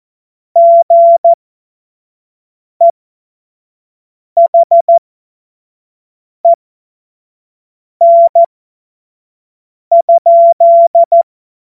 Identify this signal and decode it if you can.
Morse code, heard as a keyed tone, about 14 wpm, spaced as for 5 wpm, 680 Hz. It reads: GEHEN?
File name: Morsekurs / Witze / Witz 16 Januar 2024 25 F70